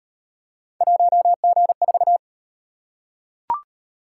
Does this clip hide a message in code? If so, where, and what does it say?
Yes: Morse code from 0.8 to 2.2 s, heard as a keyed tone, about 35 wpm, 700 Hz. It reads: 1G4